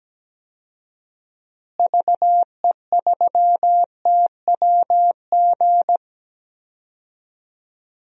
Morse code, heard as a keyed tone, about 17 wpm, 700 Hz. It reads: VE3TWG
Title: DXCC-100 Call Signs - 6 Characters - Repeated 3x - 17wpm